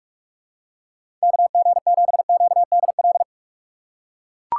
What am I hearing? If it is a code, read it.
Morse code, heard as a keyed tone, about 43 wpm, 700 Hz. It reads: KG7QDL
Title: USA Call Signs - 6 Characters - 2x Character Spacing - 45wpm